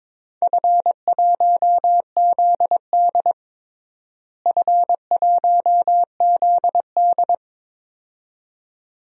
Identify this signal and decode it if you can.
Morse code, heard as a keyed tone, about 22 wpm, 700 Hz. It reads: F1ZD F1ZD
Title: DXCC-100 Call Signs - 4 Characters - Repeated 2x - 22wpm